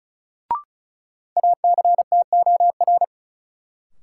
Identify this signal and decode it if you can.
Morse code, heard as a keyed tone, about 36 wpm, 700 Hz. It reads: ACTOR
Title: Sentences from Top 1500 Words - New Word Drill Alt - 35wpm